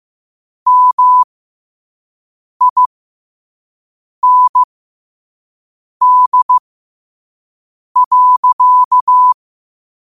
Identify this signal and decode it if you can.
Morse code, heard as a keyed tone, about 15 wpm, spaced as for 5 wpm, 1000 Hz. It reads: MIND.